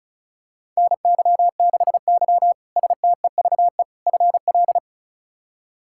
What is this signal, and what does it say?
Morse code, heard as a keyed tone, about 36 wpm, 700 Hz. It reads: NY6Y STEVE FL
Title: K1USN SST QSO Practice - 35wpm